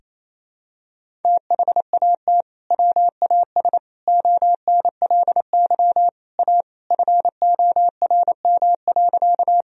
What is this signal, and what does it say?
Morse code, heard as a keyed tone, about 28 wpm, 700 Hz. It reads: THAT WAS ONLY A FORM.
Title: Sentences from Top 1000 Words - Review Part 1 - 28wpm